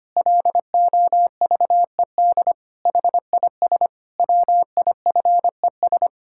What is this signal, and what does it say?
Morse code, heard as a keyed tone, about 25 wpm, 700 Hz. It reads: LOVED HIS WIFES
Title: Pangram Challenge v2 - 25wpm